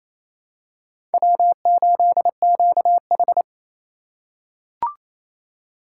Morse code, heard as a keyed tone, about 28 wpm, 700 Hz. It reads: W8QH
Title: USA Call Signs - 4 Characters - 3x Character Spacing - 28wpm